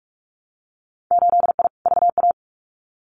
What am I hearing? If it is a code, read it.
Morse code, heard as a keyed tone, about 46 wpm, 700 Hz. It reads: 8I VA
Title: Winter Field Day Exchanges Encore 45wpm